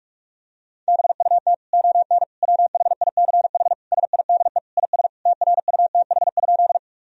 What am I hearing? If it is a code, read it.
Morse code, heard as a keyed tone, about 46 wpm, 700 Hz. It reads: BUT ON WHICH SIDE IS TRUTH?
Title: Sentences from Top 1000 Words - Review Part 1 - 45wpm